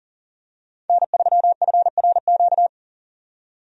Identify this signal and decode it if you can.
Morse code, heard as a keyed tone, about 40 wpm, 700 Hz. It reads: N3FRQ
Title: USA Call Signs - 5 Characters - 40wpm